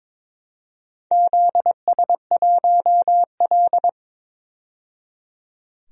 Morse code, heard as a keyed tone, about 23 wpm, 700 Hz. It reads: ZS1L
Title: DXCC-100 Call Signs - 4 Characters - 22wpm